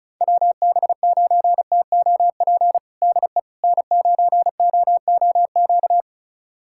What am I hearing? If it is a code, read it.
Morse code, heard as a keyed tone, about 34 wpm, 700 Hz. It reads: WB9TOP DE N9OOQ
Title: USA Callsign DE Callsign 35wpm